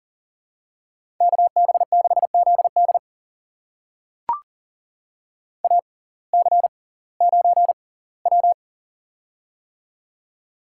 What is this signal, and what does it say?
Morse code, heard as a keyed tone, about 40 wpm, spaced as for 22 wpm, 700 Hz. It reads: KB6ZD AC9W
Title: USA Call Signs - 6x Character Spacing - 40wpm